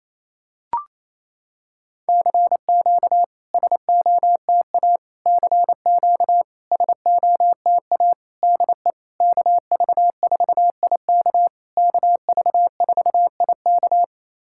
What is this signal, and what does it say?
Morse code, heard as a keyed tone, about 28 wpm, 690 Hz. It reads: CQ SOTA CQ SOTA DE KV4IK KV4IK
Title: SOTA QSO Practice 28wpm